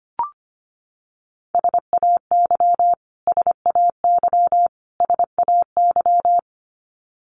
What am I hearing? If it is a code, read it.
Morse code, heard as a keyed tone, about 25 wpm, 700 Hz. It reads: SAY SAY SAY
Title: Top 500 Words Repeated 25wpm